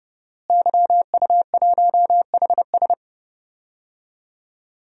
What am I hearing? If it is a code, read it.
Morse code, heard as a keyed tone, about 30 wpm, 700 Hz. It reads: YU1HS
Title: DXCC-100 Call Signs - 5 Characters - Repeated 2x - 30wpm